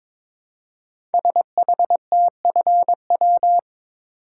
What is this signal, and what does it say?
Morse code, heard as a keyed tone, about 22 wpm, 700 Hz. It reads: SHTFW
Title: Code Groups - 5 Letters - 22wpm